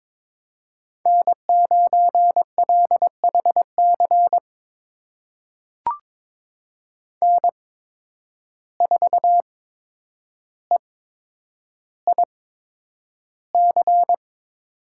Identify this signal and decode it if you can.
Morse code, heard as a keyed tone, about 22 wpm, spaced as for 9 wpm, 700 Hz. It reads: N9LHC N4EIC
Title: USA Call Signs - 5 Characters - 8x Character Spacing - 22wpm